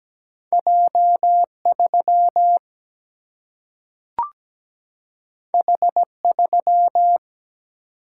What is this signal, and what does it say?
Morse code, heard as a keyed tone, about 17 wpm, 700 Hz. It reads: J3 H3
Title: Single-Letter Single-Number 17wpm